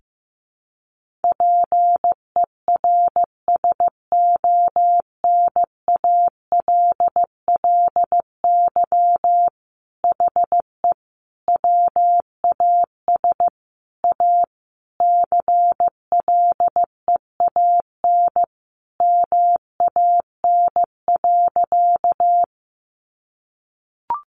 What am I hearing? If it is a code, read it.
Morse code, heard as a keyed tone, about 15 wpm, 700 Hz. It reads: PERSONALLY HE WAS A CLEAN MAN.
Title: Sentences from Top 1400 Words 15wpm